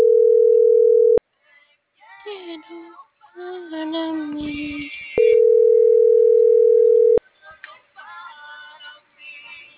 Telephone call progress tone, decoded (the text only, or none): ringback tone